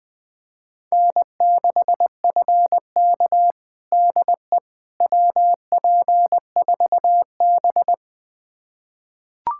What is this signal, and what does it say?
Morse code, heard as a keyed tone, about 20 wpm, 700 Hz. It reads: N6FK DE WP4B
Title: USA Callsign DE Callsign - 4 Characters - 20wpm